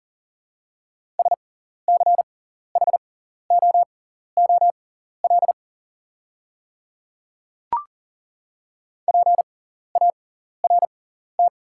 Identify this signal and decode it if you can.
Morse code, heard as a keyed tone, about 40 wpm, spaced as for 14 wpm, 700 Hz. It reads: SCHOOL PART
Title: 6x Character Spacing - Top 300 Words - 40wpm